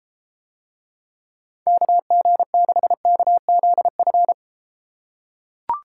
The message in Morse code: KG6KZF

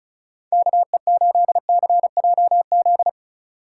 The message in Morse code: KE8CJZ

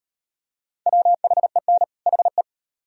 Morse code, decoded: WHEN HE

